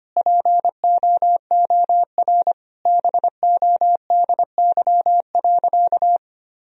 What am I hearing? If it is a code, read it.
Morse code, heard as a keyed tone, about 25 wpm, 700 Hz. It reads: POOR BODY.
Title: Sentences from Top 1000 Words - Review Part 2 - 25wpm